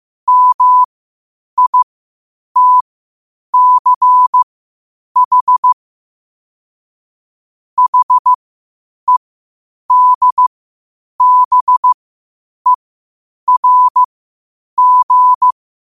Morse code, heard as a keyed tone, about 15 wpm, spaced as for 8 wpm, 1000 Hz. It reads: MITCH HEDBERG
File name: qotdfiles / 8wpm